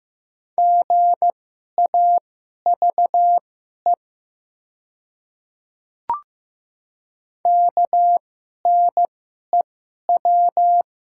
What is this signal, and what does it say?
Morse code, heard as a keyed tone, about 15 wpm, spaced as for 11 wpm, 700 Hz. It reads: GAVE KNEW
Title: 2x Character Spacing - Top 500 Words - 15wpm